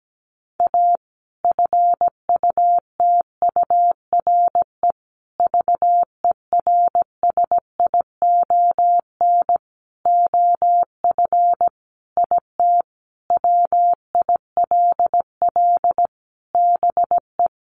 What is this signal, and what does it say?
Morse code, heard as a keyed tone, about 17 wpm, 700 Hz. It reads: A FUTURE VERSION OF IT WILL BE
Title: Sentences from Top 1400 Words 17wpm